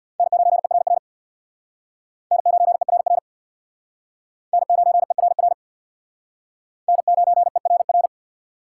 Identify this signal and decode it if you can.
Morse code, heard as a keyed tone, about 65 wpm, 700 Hz. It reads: N9ERR N9ERR N9ERR N9ERR